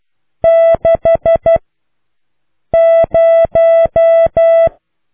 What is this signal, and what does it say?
Morse code, heard as a keyed tone, about 12 wpm, 660 Hz. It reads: 6 0